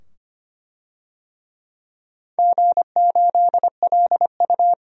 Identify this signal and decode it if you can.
Morse code, heard as a keyed tone, about 25 wpm, 700 Hz. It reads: G8LU
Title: DXCC-100 Call Signs - 4 Characters - 25wpm